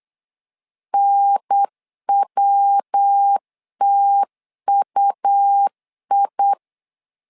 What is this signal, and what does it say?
Morse code, heard as a keyed tone, about 8 wpm, 800 Hz. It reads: NWTUI
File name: LEZIONI / 4 Q1S 8wpm 2710